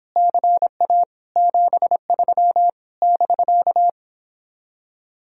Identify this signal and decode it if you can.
Morse code, heard as a keyed tone, about 31 wpm, 700 Hz. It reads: CA 73 <BK>